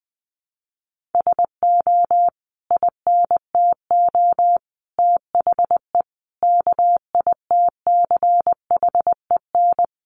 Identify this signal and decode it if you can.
Morse code, heard as a keyed tone, about 20 wpm, 700 Hz. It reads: SO INTO THE KITCHEN
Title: Sentences from Top 1400 Words 20wpm